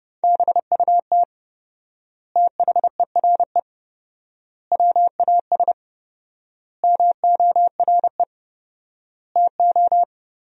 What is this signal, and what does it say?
Morse code, heard as a keyed tone, about 30 wpm, 700 Hz. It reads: BUT THERE WAS MORE TO
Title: 4x Word-Spacing - Sentences from Top 100 Words - 30wpm